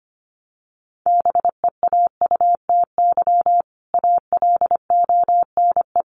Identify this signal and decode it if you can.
Morse code, heard as a keyed tone, about 25 wpm, 700 Hz. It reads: BEAUTY ALONE
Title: Sentences from Top 1200 Words 25wpm